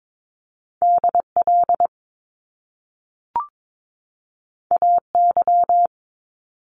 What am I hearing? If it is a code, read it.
Morse code, heard as a keyed tone, about 22 wpm, 700 Hz. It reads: DL AY
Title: Top Two Letter Combinations in Words 22wpm